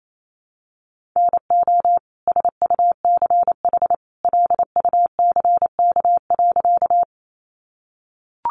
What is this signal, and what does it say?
Morse code, heard as a keyed tone, about 28 wpm, 700 Hz. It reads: NO SUCH LUCK.
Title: Sentences from Top 1500 Words 28wpm